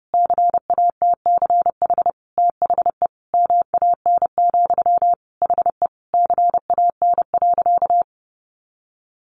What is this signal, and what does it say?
Morse code, heard as a keyed tone, about 30 wpm, 700 Hz. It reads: CATCH THE MAN, HE CAN.
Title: Sentences from Top 1500 Words 30wpm